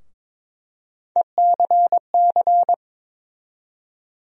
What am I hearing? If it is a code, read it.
Morse code, heard as a keyed tone, about 22 wpm, 700 Hz. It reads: ECC
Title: USA Call Sign Postfixes - 3 Letters 22wpm